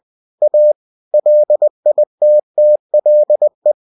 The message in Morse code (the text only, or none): A LITTLE